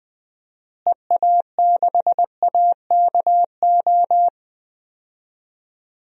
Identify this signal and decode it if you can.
Morse code, heard as a keyed tone, about 20 wpm, 700 Hz. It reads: EA6AKO
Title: DXCC-100 Call Signs - 6 Characters - Repeated 3x - 20wpm